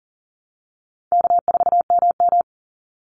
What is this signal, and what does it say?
Morse code, heard as a keyed tone, about 40 wpm, 700 Hz. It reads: K4MM
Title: Call Signs Repeated 3x - 40wpm